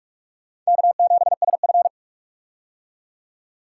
Morse code, heard as a keyed tone, about 46 wpm, 700 Hz. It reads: K7SF
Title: K1USN SST QSO Practice - 45wpm